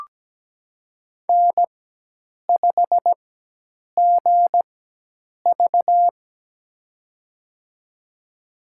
Morse code, heard as a keyed tone, about 17 wpm, spaced as for 8 wpm, 700 Hz. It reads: N5GV
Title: USA Call Signs - 4x Character Spacing - 17wpm